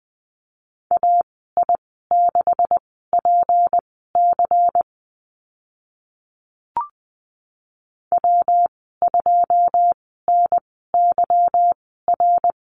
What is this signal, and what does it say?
Morse code, heard as a keyed tone, about 19 wpm, spaced as for 14 wpm, 700 Hz. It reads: AI6PC W2NYR
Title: 167 Lesson 27g - Number 7 - Call Signs 20wpm